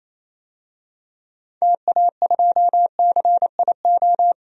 Morse code, heard as a keyed tone, about 27 wpm, 700 Hz. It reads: TA2CIO
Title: DXCC-100 Call Signs - 6 Characters - 28wpm